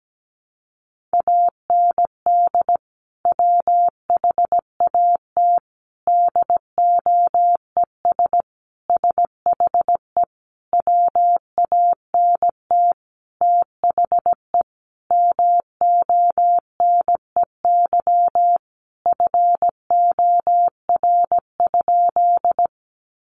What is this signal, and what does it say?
Morse code, heard as a keyed tone, about 17 wpm, 700 Hz. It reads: AND WHAT DOES SHE WANT THE MONEY FOR?
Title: Sentences from Top 400 Words 17wpm